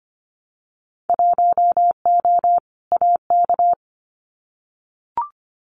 Morse code, heard as a keyed tone, about 25 wpm, 700 Hz. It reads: 1O AK
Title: Winter Field Day Exchanges Encore 25wpm